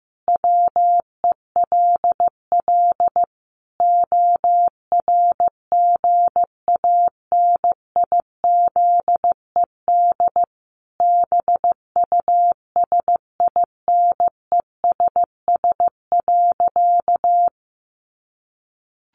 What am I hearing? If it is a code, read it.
Morse code, heard as a keyed tone, about 15 wpm, 700 Hz. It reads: WELL ORGANIZED BUSINESS.
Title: Sentences from Top 1300 Words 15wpm